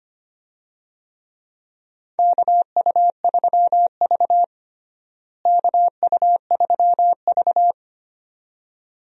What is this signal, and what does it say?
Morse code, heard as a keyed tone, about 25 wpm, 700 Hz. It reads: KU3V KU3V
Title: USA Call Signs - 4 Characters - Repeated 2x - 25wpm